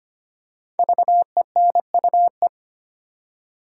Morse code, heard as a keyed tone, about 25 wpm, 700 Hz. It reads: VENUE
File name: 5 Letter Words - First 3 Letters Given - 25wpm